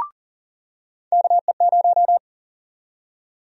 Morse code, heard as a keyed tone, about 40 wpm, 700 Hz. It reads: KE0
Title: USA Call Sign Prefixes - 40wpm